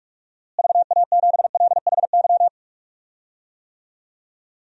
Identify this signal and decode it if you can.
Morse code, heard as a keyed tone, about 45 wpm, 700 Hz. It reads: VA7LHY
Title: DXCC-100 Call Signs - 6 Characters - Repeated 3x - 45wpm